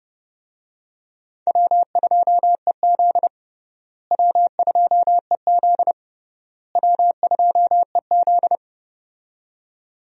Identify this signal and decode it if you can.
Morse code, heard as a keyed tone, about 29 wpm, 700 Hz. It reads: W2EZ W2EZ W2EZ
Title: USA Call Signs - 4 Characters - Repeated 3x - 30wpm